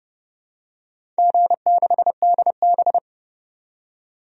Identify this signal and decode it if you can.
Morse code, heard as a keyed tone, about 30 wpm, 700 Hz. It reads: G6DB
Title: DXCC-100 Call Signs - 4 Characters - 30wpm